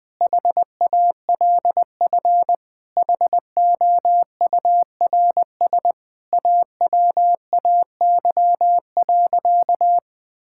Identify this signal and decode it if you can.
Morse code, heard as a keyed tone, about 21 wpm, 700 Hz. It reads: HALF HOURS AWAY.
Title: Sentences with 4-Word Phrases - Spoken Beginning - 20wpm